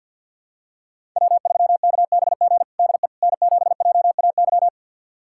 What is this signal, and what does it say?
Morse code, heard as a keyed tone, about 53 wpm, 700 Hz. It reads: W3KBG DE N7JAY
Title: USA Callsign DE Callsign - 5 Characters - 50wpm